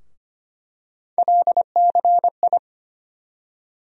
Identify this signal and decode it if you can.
Morse code, heard as a keyed tone, about 25 wpm, 700 Hz. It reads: LCI
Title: USA Call Sign Postfixes - 3 Letters 25wpm